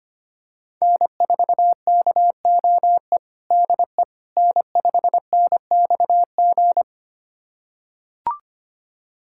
Morse code, heard as a keyed tone, about 25 wpm, 700 Hz. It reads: N4KOE DE N5NXG
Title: USA Callsign DE Callsign - 5 Characters - 25wpm